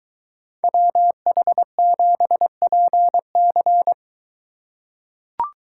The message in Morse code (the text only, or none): WH7PC